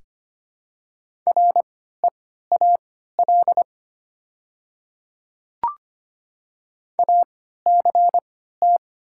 Morse code, heard as a keyed tone, about 25 wpm, spaced as for 14 wpm, 700 Hz. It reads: REAL ACT
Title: 3x Character Spacing - Top 300 Words - 25wpm